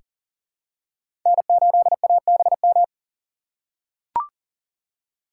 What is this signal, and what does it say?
Morse code, heard as a keyed tone, about 40 wpm, 700 Hz. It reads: N8ABM
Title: USA Call Signs - 5 Characters - 3x Character Spacing - 40wpm